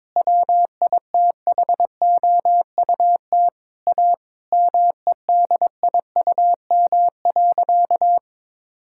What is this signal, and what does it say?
Morse code, heard as a keyed tone, about 22 wpm, 700 Hz. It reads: WITHOUT A MEDIUM.
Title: Sentences from Top 1000 Words - Review Part 6 - 22wpm